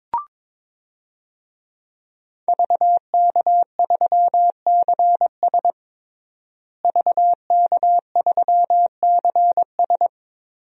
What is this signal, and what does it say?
Morse code, heard as a keyed tone, about 22 wpm, 700 Hz. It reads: VK3CS VK3CS
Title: DXCC-100 Call Signs - 5 Characters - Repeated 2x - 22wpm